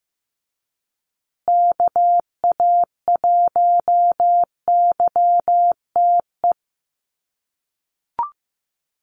Morse code, heard as a keyed tone, about 15 wpm, 700 Hz. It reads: KA1YTE
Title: USA Call Signs Repeated 3x 15wpm